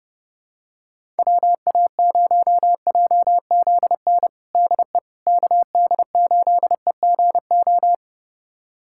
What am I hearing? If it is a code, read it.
Morse code, heard as a keyed tone, about 32 wpm, 700 Hz. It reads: WA0JZN DE KD8EGO
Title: USA Callsign DE Callsign 30wpm